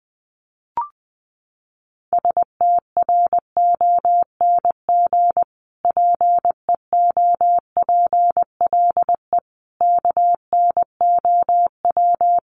Morse code, heard as a keyed tone, about 20 wpm, 700 Hz. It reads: STRONG PEOPLE KNOW
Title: Sentences from Top 1500 Words 20wpm